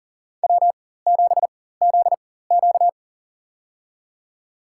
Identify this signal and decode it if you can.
Morse code, heard as a keyed tone, about 40 wpm, spaced as for 19 wpm, 700 Hz. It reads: W7ZQ
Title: USA Call Signs - 4 Characters - 4x Character Spacing - 40wpm